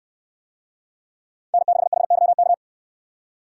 Morse code, heard as a keyed tone, about 69 wpm, 700 Hz. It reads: N6HYK